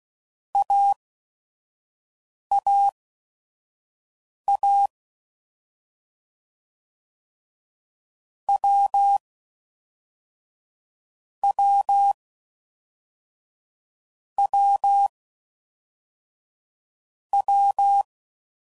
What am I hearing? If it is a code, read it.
Morse code, heard as a keyed tone, about 16 wpm, spaced as for 5 wpm, 800 Hz. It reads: AAA W W W W